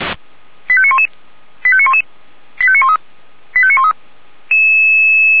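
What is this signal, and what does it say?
Five-tone selective call (ZVEI), heard as quick runs of five tones, tones of about 70 ms. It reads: 85911 85911 85913 85913
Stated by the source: ZVEI five-tone call received on an SDR.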